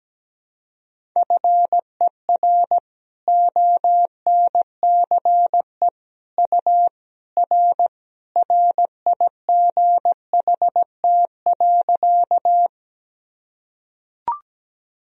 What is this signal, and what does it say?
Morse code, heard as a keyed tone, about 17 wpm, 700 Hz. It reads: FER ONCE U R RIGHT.
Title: Sentences from Top 300 Words and Abbreviations 17wpm